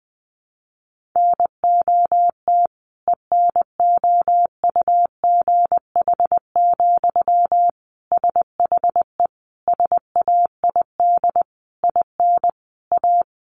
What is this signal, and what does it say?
Morse code, heard as a keyed tone, about 20 wpm, 700 Hz. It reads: NOT ENOUGH, SHE SAID IN A
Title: Sentences from Top 1500 Words 20wpm